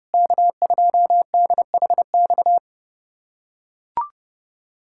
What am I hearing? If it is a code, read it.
Morse code, heard as a keyed tone, about 30 wpm, 700 Hz. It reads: K2DHX
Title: USA Callsign DE Callsign - 5 Characters - 30wpm